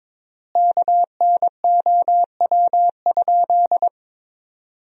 Morse code, heard as a keyed tone, about 22 wpm, 700 Hz. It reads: KNOW?
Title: 3x Word-Spacing - Sentences from Top 200 Words - 22wpm